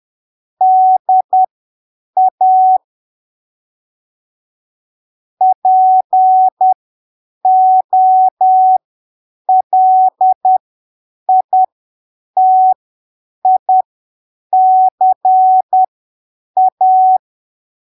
Morse code, heard as a keyed tone, about 10 wpm, spaced as for 7 wpm, 750 Hz. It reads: DA POLITICA
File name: arquivos / texto / texto 24 10ppm